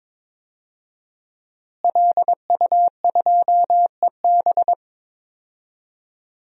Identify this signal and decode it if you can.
Morse code, heard as a keyed tone, about 22 wpm, 700 Hz. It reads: LU2EB
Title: DXCC-100 Call Signs - 5 Characters - Repeated 3x - 22wpm